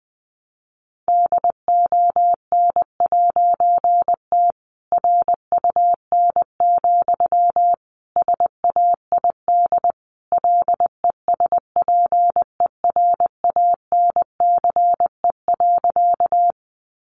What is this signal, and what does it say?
Morse code, heard as a keyed tone, about 20 wpm, 700 Hz. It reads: DON'T RUN, SAID LESPERANCE.
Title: A Sound Of Thunder 20wpm